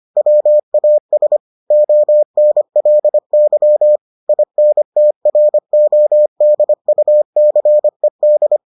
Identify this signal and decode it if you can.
Morse code, heard as a keyed tone, about 25 wpm, 600 Hz. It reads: WAS ONLY INTRODUCED